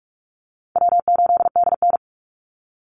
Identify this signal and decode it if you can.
Morse code, heard as a keyed tone, about 44 wpm, 700 Hz. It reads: W8DN
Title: Easy Call Signs Repeated 2x - 45wpm